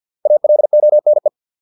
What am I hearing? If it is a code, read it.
Morse code, heard as a keyed tone, about 50 wpm, 600 Hz. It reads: ALONE